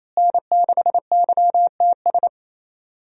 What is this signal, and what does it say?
Morse code, heard as a keyed tone, about 28 wpm, 700 Hz. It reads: N6YTS
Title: USA Call Signs - 5 Characters - Repeated 3x - 28wpm